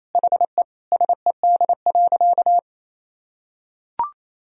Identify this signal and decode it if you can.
Morse code, heard as a keyed tone, about 28 wpm, 700 Hz. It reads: HE SED.
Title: Sentences from Top 200 Words and Abbreviations 28wpm